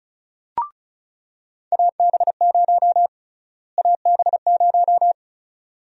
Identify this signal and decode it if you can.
Morse code, heard as a keyed tone, about 36 wpm, 700 Hz. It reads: AB0 AB0
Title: USA Call Sign Prefixes - Repeated 2x - 35wpm